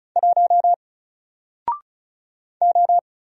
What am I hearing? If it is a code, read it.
Morse code, heard as a keyed tone, about 35 wpm, 700 Hz. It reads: 1 O
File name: Single-Letter-Number-35wpm